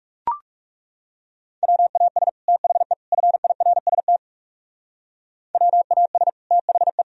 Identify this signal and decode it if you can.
Morse code, heard as a keyed tone, about 38 wpm, 700 Hz. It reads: WAS THE FIRST WAS THE
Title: Speed-Racing - Sets of 3 Words 30wpm